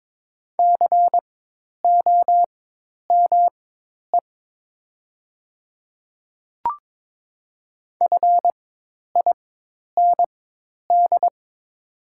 Morse code, heard as a keyed tone, about 22 wpm, spaced as for 10 wpm, 700 Hz. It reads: COME FIND
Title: 4x Character Spacing - Top 200 Words - 22wpm